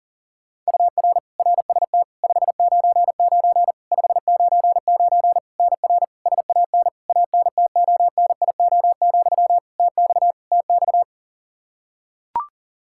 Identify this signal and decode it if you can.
Morse code, heard as a keyed tone, about 37 wpm, 700 Hz. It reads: UR RST 599 599 NR SAN ANTONIO, TX TX
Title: K1USN SST QSO Practice 40wpm